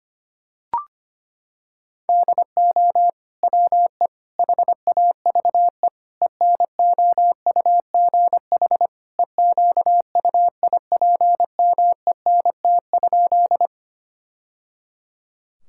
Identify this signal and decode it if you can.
Morse code, heard as a keyed tone, about 25 wpm, 700 Hz. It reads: DO WE HAVE ENOUGH EQUIPMENT?
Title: Sentences from Top 1000 Words - Review Part 5 - 25wpm